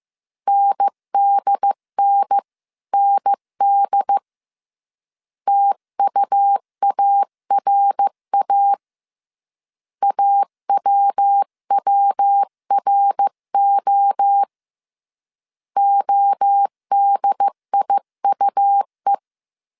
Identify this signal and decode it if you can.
Morse code, heard as a keyed tone, about 15 wpm, 800 Hz. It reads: NDNND TUARA AWWRO ODIUE
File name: CW / DET / 1 3 1500C S A 14wpm 2447